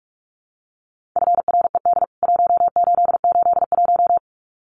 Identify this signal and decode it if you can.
Morse code, heard as a keyed tone, about 44 wpm, 700 Hz. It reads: FRED 1881
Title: CWT Exchange 45wpm